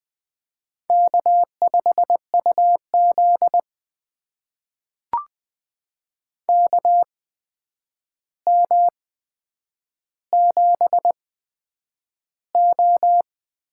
Morse code, heard as a keyed tone, about 20 wpm, spaced as for 9 wpm, 700 Hz. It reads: K5UZ KM7O
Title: USA Call Signs - 4 Characters - 8x Character Spacing - 20wpm